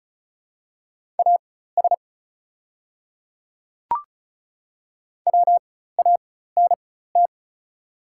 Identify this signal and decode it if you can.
Morse code, heard as a keyed tone, about 36 wpm, spaced as for 16 wpm, 700 Hz. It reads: AS WANT
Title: 4x Character Spacing - Top 300 Words - 35wpm